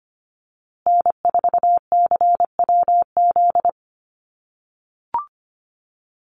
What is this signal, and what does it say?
Morse code, heard as a keyed tone, about 25 wpm, 700 Hz. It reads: N4CWZ
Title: USA Call Signs Repeated 3x 25wpm